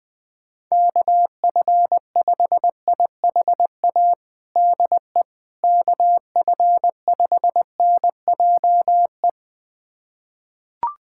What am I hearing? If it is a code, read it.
Morse code, heard as a keyed tone, about 20 wpm, 700 Hz. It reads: KF5IHA DE KF5NJE
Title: USA Callsign DE Callsign 20wpm